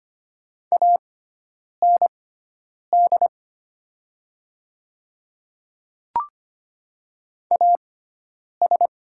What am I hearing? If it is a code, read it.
Morse code, heard as a keyed tone, about 25 wpm, spaced as for 9 wpm, 700 Hz. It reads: AND AS
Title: 6x Character Spacing - Top 100 Words - 25wpm